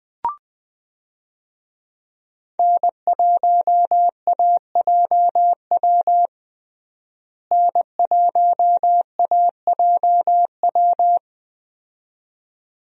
Morse code, heard as a keyed tone, about 21 wpm, 700 Hz. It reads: N1AJW N1AJW
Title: USA Call Signs - 5 Characters - Repeated 2x - 20wpm